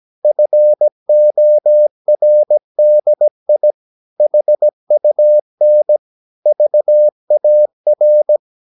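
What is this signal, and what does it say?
Morse code, heard as a keyed tone, about 17 wpm, 600 Hz. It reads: FORDI HUN VAR
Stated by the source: Slow (QRS) Morse practice at 17 wpm, from CW med Gnister.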